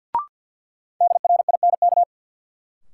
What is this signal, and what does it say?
Morse code, heard as a keyed tone, about 53 wpm, 700 Hz. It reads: DRINK
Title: Sentences from Top 1500 Words - New Word Drill Alt - 50wpm